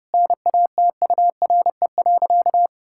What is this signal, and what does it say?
Morse code, heard as a keyed tone, about 28 wpm, 700 Hz. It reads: NATURE.
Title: Sentences from Top 1000 Words - Review Part 6 - 30wpm